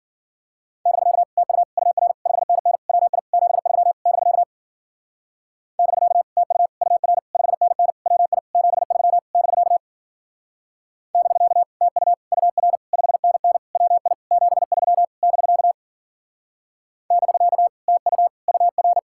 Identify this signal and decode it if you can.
Morse code, heard as a keyed tone, about 49 wpm, 700 Hz. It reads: <BK> TU UR 5NN WI 73 <BK> <BK> TU UR 5NN WI 73 <BK> <BK> TU UR 5NN WI 73 <BK> <BK> TU UR
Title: Speed-Racing Common QSO Phrases - 40wpm